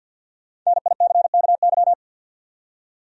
Morse code, heard as a keyed tone, about 53 wpm, 700 Hz. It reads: NICKY
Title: USA Common Names - 50wpm